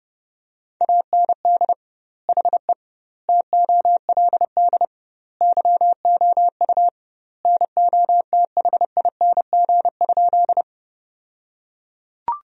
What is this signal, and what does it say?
Morse code, heard as a keyed tone, about 30 wpm, 700 Hz. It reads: AND HE TOLD YOU NOTHING?
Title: 2x Word-Spacing - Sentences from Top 500 Words - 30wpm